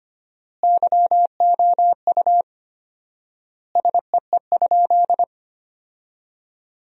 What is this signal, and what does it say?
Morse code, heard as a keyed tone, about 25 wpm, 700 Hz. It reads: YOU SEE?
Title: 4x Word-Spacing - Sentences from Top 100 Words - 25wpm